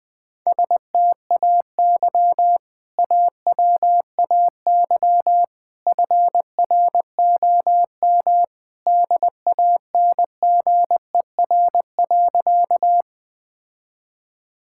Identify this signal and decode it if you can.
Morse code, heard as a keyed tone, about 21 wpm, 700 Hz. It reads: STAY AWAY FROM DANGER.
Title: Sentences with 2-Word Phrases - 20wpm